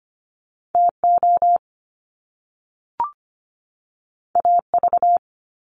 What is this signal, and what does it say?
Morse code, heard as a keyed tone, about 25 wpm, 700 Hz. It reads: T O A V
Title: Top Two Letter Combinations in Words 25wpm